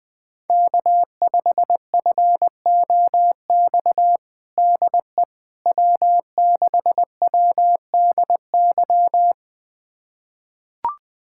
Morse code, heard as a keyed tone, about 20 wpm, 700 Hz. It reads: K5FOX DE W6WDY
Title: USA Callsign DE Callsign - 5 Characters - 20wpm